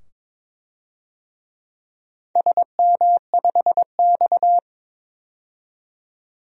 Morse code, heard as a keyed tone, about 22 wpm, 700 Hz. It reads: SM5X
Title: DXCC-100 Call Signs - 4 Characters - 22wpm